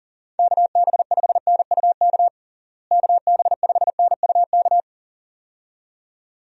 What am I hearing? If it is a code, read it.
Morse code, heard as a keyed tone, about 37 wpm, 700 Hz. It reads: KB5NUK KB5NUK
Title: USA Call Signs - 6 Characters - Repeated 2x - 40wpm